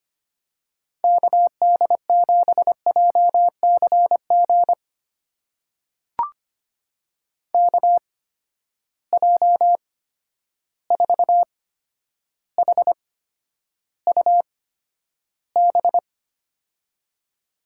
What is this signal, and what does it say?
Morse code, heard as a keyed tone, about 25 wpm, spaced as for 11 wpm, 700 Hz. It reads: KD7JCG KJ4HUB
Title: USA Call Signs - 6 Characters - 8x Character Spacing - 25wpm